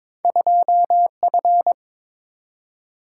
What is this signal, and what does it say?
Morse code, heard as a keyed tone, about 22 wpm, 700 Hz. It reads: 2F
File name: practice / 22wpm / Two Letter-Number Combinations - 22wpm